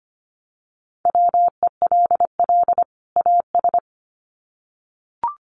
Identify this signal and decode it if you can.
Morse code, heard as a keyed tone, about 25 wpm, 700 Hz. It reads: WELL AS
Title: Sets of 2 Words 25wpm